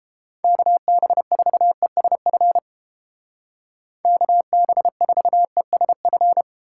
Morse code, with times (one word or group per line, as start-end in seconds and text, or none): 0.4-2.6 KB4ESF
4.0-6.4 KB4ESF